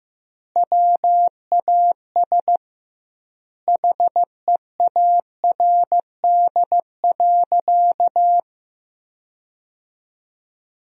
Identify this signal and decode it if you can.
Morse code, heard as a keyed tone, about 15 wpm, 700 Hz. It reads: WAS HEARD.
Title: 2x Word-Spacing - Sentences from Top 400 Words - 15wpm